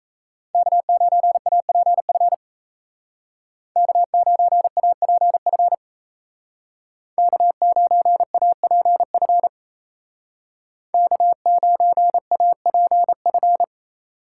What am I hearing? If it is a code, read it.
Morse code, heard as a keyed tone, about 36 wpm, 700 Hz. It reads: K9APF K9APF K9APF K9APF